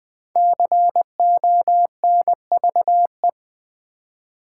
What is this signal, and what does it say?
Morse code, heard as a keyed tone, about 20 wpm, 700 Hz. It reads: CONVE
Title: Build A Word - 5 Letters At A Time - 12-Letter Words - 20wpm